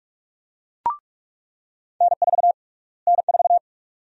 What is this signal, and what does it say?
Morse code, heard as a keyed tone, about 46 wpm, 700 Hz. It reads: N4 N4
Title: USA Call Sign Prefixes - Repeated 2x - 45wpm